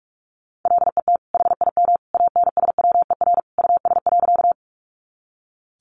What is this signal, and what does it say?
Morse code, heard as a keyed tone, about 45 wpm, 700 Hz. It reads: LET HIM ANSWER US.
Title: Sentences from Top 300 Words 45wpm